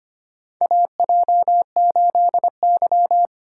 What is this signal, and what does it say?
Morse code, heard as a keyed tone, about 25 wpm, 700 Hz. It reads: AJ8Y
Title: USA Call Signs - 2x Character Spacing - 25wpm